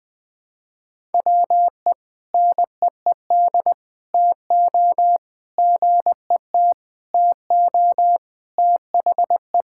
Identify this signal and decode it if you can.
Morse code, heard as a keyed tone, about 20 wpm, 700 Hz. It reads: WE NEED TO GET TO THE
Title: Sentences with 3-Word Phrases - 20wpm